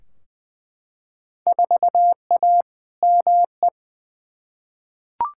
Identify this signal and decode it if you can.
Morse code, heard as a keyed tone, about 20 wpm, 700 Hz. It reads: 4A ME
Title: Field Day Exchanges - 20wpm